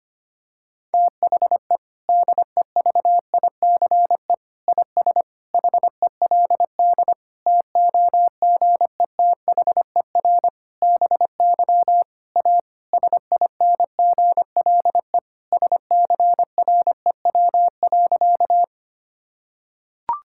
Morse code, heard as a keyed tone, about 25 wpm, 700 Hz. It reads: THE DEVICE IS HELD TOGETHER BY A SINGLE SCREW.
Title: Sentences from Top 1600 Words 25wpm